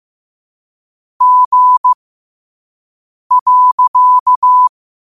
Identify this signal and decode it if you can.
Morse code, heard as a keyed tone, about 15 wpm, spaced as for 5 wpm, 1000 Hz. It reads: G.
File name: qotdfiles / 5wpm